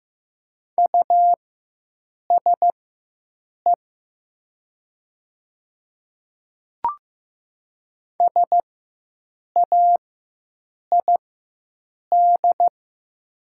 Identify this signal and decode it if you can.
Morse code, heard as a keyed tone, about 15 wpm, spaced as for 7 wpm, 700 Hz. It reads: USE SAID
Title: 4x Character Spacing - Top 100 Words - 15wpm